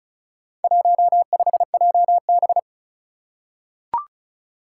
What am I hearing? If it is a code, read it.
Morse code, heard as a keyed tone, about 34 wpm, 700 Hz. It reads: 15JB
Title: Four Challenging Letter-Number Combinations 35wpm